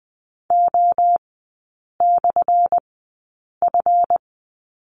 Morse code, heard as a keyed tone, about 20 wpm, spaced as for 8 wpm, 700 Hz. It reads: O/F
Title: 178 Lesson 29e - Slash - Three Characters 20wpm